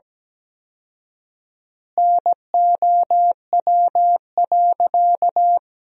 Morse code, heard as a keyed tone, about 17 wpm, 700 Hz. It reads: NOW.
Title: 4x Word-Spacing - Sentences from Top 300 Words - 17wpm